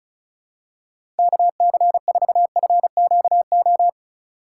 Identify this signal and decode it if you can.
Morse code, heard as a keyed tone, about 36 wpm, 700 Hz. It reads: KC4FQO
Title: USA Call Signs - 6 Characters - 4x Character Spacing - 35wpm